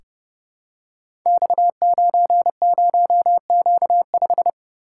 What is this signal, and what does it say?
Morse code, heard as a keyed tone, about 30 wpm, 700 Hz. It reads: X90Q5
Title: Code Groups - 5 Letter:Numbers - 30wpm